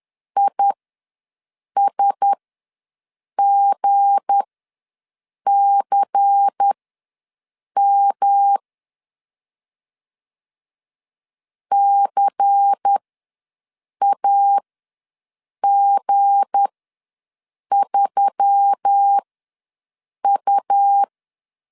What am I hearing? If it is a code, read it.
Morse code, heard as a keyed tone, about 10 wpm, spaced as for 6 wpm, 800 Hz. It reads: ISGCM CAG3U
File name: LEZIONI / 7 GF9Z / 7 G ws 10wpm 3843